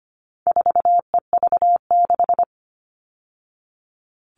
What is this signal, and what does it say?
Morse code, heard as a keyed tone, about 25 wpm, 700 Hz. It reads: 4EV6
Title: All Cut Numbers - 4-Digit 25wpm